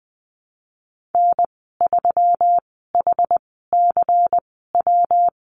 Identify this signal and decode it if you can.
Morse code, heard as a keyed tone, about 20 wpm, spaced as for 15 wpm, 700 Hz. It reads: N3HCW
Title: 122 Lesson 20g - Number 3 - Call Signs 20wpm